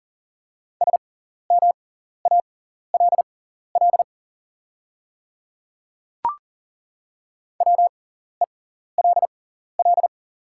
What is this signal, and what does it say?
Morse code, heard as a keyed tone, about 40 wpm, spaced as for 14 wpm, 700 Hz. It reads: SMALL WELL
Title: 6x Character Spacing - Top 300 Words - 40wpm